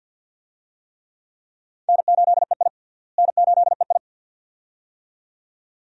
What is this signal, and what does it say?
Morse code, heard as a keyed tone, about 46 wpm, 700 Hz. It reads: N8EI N8EI
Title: USA Call Signs - 4 Characters - Repeated 2x - 50wpm